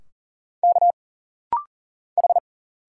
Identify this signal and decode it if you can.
Morse code, heard as a keyed tone, about 40 wpm, 700 Hz. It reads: K H
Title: Single Letters - Rapid-Fire 40wpm